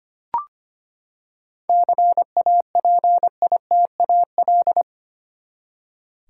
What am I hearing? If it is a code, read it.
Morse code, heard as a keyed tone, about 25 wpm, 700 Hz. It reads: CAPITAL
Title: Seven Letter Words - 25wpm